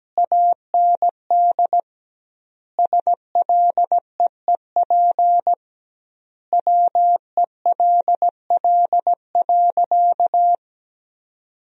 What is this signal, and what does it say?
Morse code, heard as a keyed tone, about 17 wpm, 700 Hz. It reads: AND SLEEP WELL.
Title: 2x Word-Spacing - Sentences from Top 500 Words - 17wpm